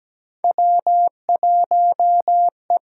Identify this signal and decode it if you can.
Morse code, heard as a keyed tone, about 17 wpm, 700 Hz. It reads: W1E
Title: USA Call Signs - 8x Character Spacing - 17wpm